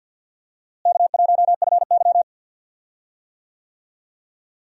Morse code, heard as a keyed tone, about 50 wpm, 700 Hz. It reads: K1FY
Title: USA Call Signs - 4 Characters - Repeated 2x - 50wpm